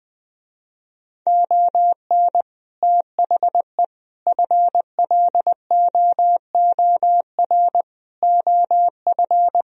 Morse code, heard as a keyed tone, about 20 wpm, 700 Hz. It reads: ON THE FLOOR OF
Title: Speed-Racing - Sets of 5 Words 17wpm